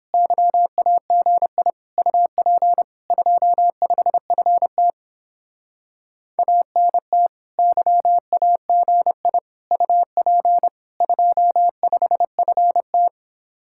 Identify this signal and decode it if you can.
Morse code, heard as a keyed tone, about 29 wpm, 700 Hz. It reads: YAGI UP 25FT ANT YAGI UP 25FT